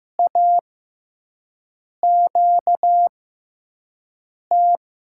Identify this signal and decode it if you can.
Morse code, heard as a keyed tone, about 15 wpm, spaced as for 5 wpm, 700 Hz. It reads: AQT
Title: USA Call Signs - 5 Characters - 6x Character Spacing - 15wpm